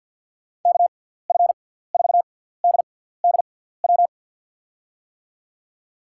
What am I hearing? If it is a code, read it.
Morse code, heard as a keyed tone, about 52 wpm, 700 Hz. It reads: K F 4 D D W
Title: USA Call Signs - 6 Characters - 6x Character Spacing - 50wpm